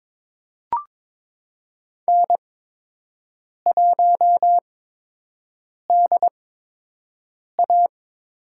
Morse code, heard as a keyed tone, about 22 wpm, spaced as for 6 wpm, 700 Hz. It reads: N1DA
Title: USA Call Signs - 8x Character Spacing - 22wpm